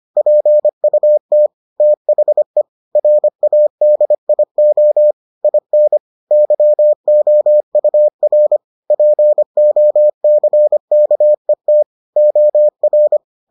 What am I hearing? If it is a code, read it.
Morse code, heard as a keyed tone, about 25 wpm, 600 Hz. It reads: PUT THE RADIO IN YOUR POCKET OR